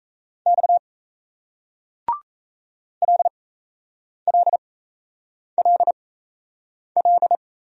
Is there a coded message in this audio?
Yes: Morse code from 0.5 to 7.4 s, heard as a keyed tone, about 36 wpm, spaced as for 8 wpm, 700 Hz. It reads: X LLLL